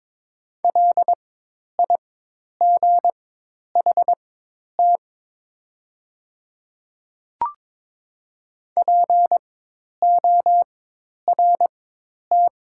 Morse code, heard as a keyed tone, about 22 wpm, spaced as for 10 wpm, 700 Hz. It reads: LIGHT PORT
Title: 4x Character Spacing - Top 200 Words - 22wpm